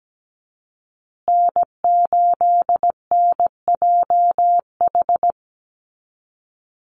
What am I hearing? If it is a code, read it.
Morse code, heard as a keyed tone, about 17 wpm, 700 Hz. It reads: N8NJH
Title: USA Call Signs Repeated 3x 17wpm